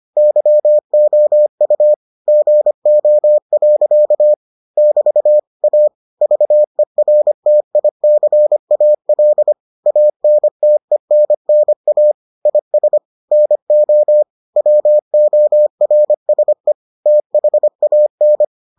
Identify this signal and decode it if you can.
Morse code, heard as a keyed tone, about 25 wpm, 600 Hz. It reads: YOU GO. <BT> A VERTICAL ANTENNA IS NO WORSE THAN